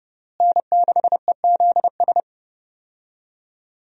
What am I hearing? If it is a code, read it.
Morse code, heard as a keyed tone, about 30 wpm, 700 Hz. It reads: N6EZS
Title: USA Call Signs - 5 Characters - Repeated 2x - 30wpm